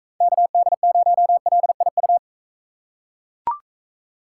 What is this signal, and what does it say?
Morse code, heard as a keyed tone, about 44 wpm, 700 Hz. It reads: KD0LIU